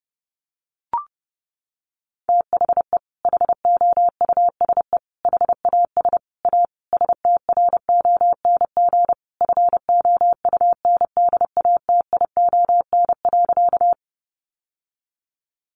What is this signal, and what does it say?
Morse code, heard as a keyed tone, about 30 wpm, 700 Hz. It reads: THE HOUSE HAS A STRONG FOUNDATION.
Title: Sentences from Top 1200 Words 30wpm